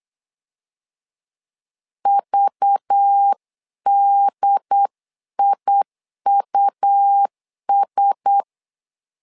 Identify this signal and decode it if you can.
Morse code, heard as a keyed tone, about 8 wpm, 800 Hz. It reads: VDIUS
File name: LEZIONI / 5 LBVC / 5 LBV 8wpm 2837